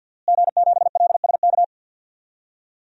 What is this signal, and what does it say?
Morse code, heard as a keyed tone, about 52 wpm, 700 Hz. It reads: G7LSK
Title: DXCC-100 Call Signs - 5 Characters - Repeated 2x - 50wpm